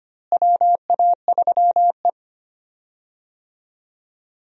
Morse code, heard as a keyed tone, about 25 wpm, 700 Hz. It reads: WA3E
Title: USA Call Signs - 4 Characters - Repeated 2x - 25wpm